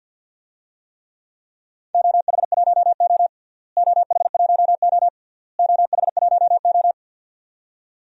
Morse code, heard as a keyed tone, about 47 wpm, 700 Hz. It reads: OH1O OH1O OH1O
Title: DXCC-100 Call Signs - 4 Characters - Repeated 3x - 50wpm